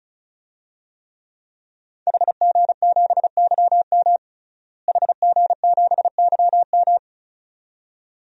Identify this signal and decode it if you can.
Morse code, heard as a keyed tone, about 35 wpm, 700 Hz. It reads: HG7YM HG7YM
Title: DXCC-100 Call Signs - 5 Characters - Repeated 2x - 35wpm